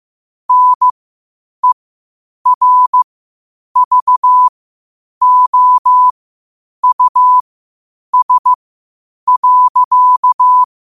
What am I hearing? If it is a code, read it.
Morse code, heard as a keyed tone, about 15 wpm, spaced as for 8 wpm, 1000 Hz. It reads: NERVOUS.
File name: qotdfiles / 8wpm